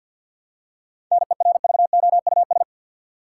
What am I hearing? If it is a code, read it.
Morse code, heard as a keyed tone, about 50 wpm, 700 Hz. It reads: NERVOUS
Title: Seven Letter Words - 50wpm